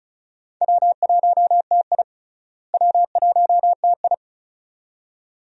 Morse code, heard as a keyed tone, about 36 wpm, 700 Hz. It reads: W1TI W1TI
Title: USA Call Signs - 4 Characters - Repeated 2x - 35wpm